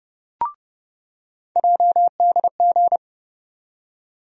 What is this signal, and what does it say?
Morse code, heard as a keyed tone, about 31 wpm, 700 Hz. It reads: JDG
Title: USA Call Sign Postfixes - 3 Letters 30wpm